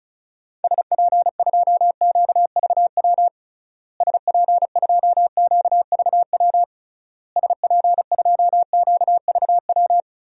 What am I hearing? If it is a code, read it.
Morse code, heard as a keyed tone, about 36 wpm, 700 Hz. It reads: SP2QVW SP2QVW SP2QVW
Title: DXCC-100 Call Signs - 6 Characters - Repeated 3x - 35wpm